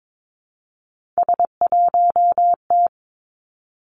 Morse code, heard as a keyed tone, about 22 wpm, 700 Hz. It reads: S1T
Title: Three Challenging Letter-Number Combinations 22wpm